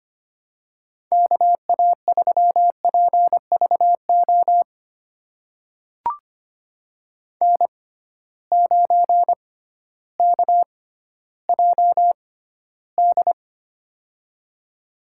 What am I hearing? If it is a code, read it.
Morse code, heard as a keyed tone, about 25 wpm, spaced as for 13 wpm, 700 Hz. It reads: KA3PVO N9KJD
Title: USA Call Signs - 6x Character Spacing - 25wpm